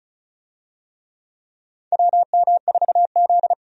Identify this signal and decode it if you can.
Morse code, heard as a keyed tone, about 34 wpm, 700 Hz. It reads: WM4Z
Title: USA Call Signs - 4 Characters - 35wpm